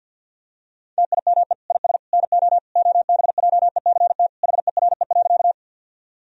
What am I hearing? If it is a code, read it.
Morse code, heard as a keyed tone, about 52 wpm, 700 Hz. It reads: TIME IS NO OBJECT HERE.
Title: Sentences from Top 1000 Words - Review Part 3 - 50wpm